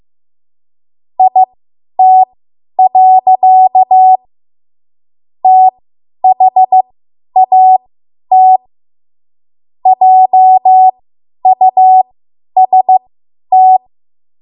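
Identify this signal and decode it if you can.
Morse code, heard as a keyed tone, about 15 wpm, spaced as for 10 wpm, 750 Hz. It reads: IT. THAT JUST
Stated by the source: Morse code practice run at 10 WPM from a code practice archive.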